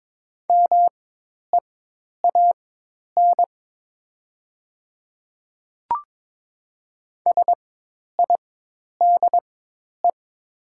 Morse code, heard as a keyed tone, about 22 wpm, spaced as for 10 wpm, 700 Hz. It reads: MEAN SIDE
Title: 4x Character Spacing - Top 200 Words - 22wpm